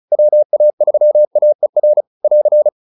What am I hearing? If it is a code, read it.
Morse code, heard as a keyed tone, about 34 wpm, 590 Hz. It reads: WA3AER <AR>
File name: POTA306 35